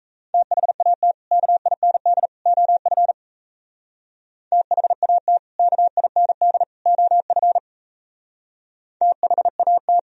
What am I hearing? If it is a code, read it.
Morse code, heard as a keyed tone, about 34 wpm, 700 Hz. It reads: THAT KIND OF THAT KIND OF THAT
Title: Speed-Racing - Sets of 3 Words 28wpm